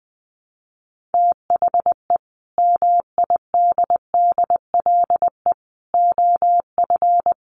Morse code, heard as a keyed tone, about 20 wpm, 700 Hz. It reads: THE MIDDLE OF
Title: Sets of 3 Words 20wpm